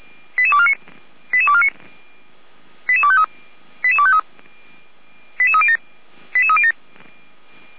Five-tone selective call (ZVEI), heard as quick runs of five tones, tones of about 70 ms. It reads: 88249 88249 88253 88253 88307 88307